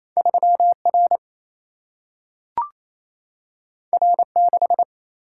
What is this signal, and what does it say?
Morse code, heard as a keyed tone, about 28 wpm, 700 Hz. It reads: 3 R R 6
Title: Single-Letter Single-Number 28wpm